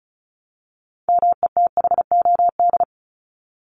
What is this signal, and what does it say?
Morse code, heard as a keyed tone, about 34 wpm, 700 Hz. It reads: METHOD
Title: Top 501-750 Words 35wpm